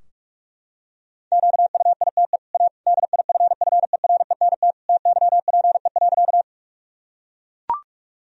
Morse code, heard as a keyed tone, about 41 wpm, 700 Hz. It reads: QUITE A DIFFERENT TYPE.
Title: Sentences from Top 1000 Words - Review Part 3 - 45wpm